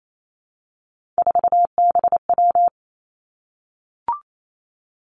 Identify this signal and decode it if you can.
Morse code, heard as a keyed tone, about 28 wpm, 700 Hz. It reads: 4BW
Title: Three Challenging Letter-Number Combinations 28wpm